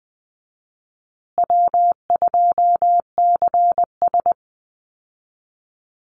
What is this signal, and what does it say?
Morse code, heard as a keyed tone, about 19 wpm, 700 Hz. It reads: W2CS
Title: Easy Call Signs Repeated 3x - 20wpm